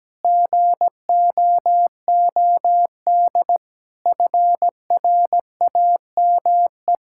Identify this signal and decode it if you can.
Morse code, heard as a keyed tone, about 17 wpm, 700 Hz. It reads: GOOD FRAME
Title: Sentences from Top 1000 Words - Review Part 6 - 17wpm